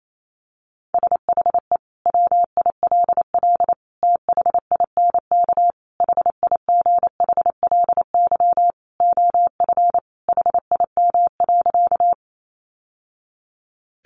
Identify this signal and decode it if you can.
Morse code, heard as a keyed tone, about 28 wpm, 700 Hz. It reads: SHE WILL THINK HIGHLY OF HIM.
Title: Sentences from Top 1100 Words 28wpm